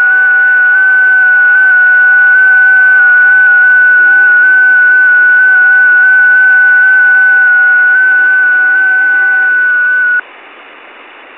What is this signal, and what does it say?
FT8.
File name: test / websdr test17